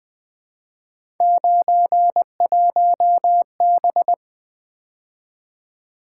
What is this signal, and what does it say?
Morse code, heard as a keyed tone, about 23 wpm, 700 Hz. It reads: 91B
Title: Speed-Racing - Three Letter-Number Combinations - 20wpm